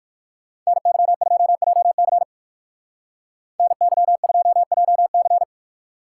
Morse code, heard as a keyed tone, about 56 wpm, 700 Hz. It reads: NY2JC NY2JC
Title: Speed Racing - USA Call Signs 45wpm